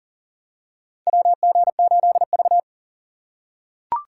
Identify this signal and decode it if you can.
Morse code, heard as a keyed tone, about 40 wpm, 700 Hz. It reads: WG8V
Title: USA Call Signs - 2x Character Spacing - 40wpm